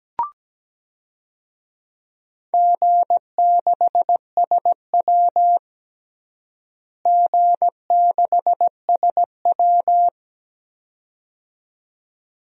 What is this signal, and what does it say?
Morse code, heard as a keyed tone, about 17 wpm, 700 Hz. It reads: G6SW G6SW
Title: DXCC-100 Call Signs - 4 Characters - Repeated 2x - 17wpm